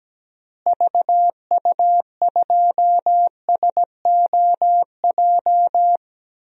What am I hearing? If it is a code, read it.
Morse code, heard as a keyed tone, about 17 wpm, 700 Hz. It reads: VU2SOJ
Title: DXCC-100 Call Signs - 6 Characters - Repeated 2x - 17wpm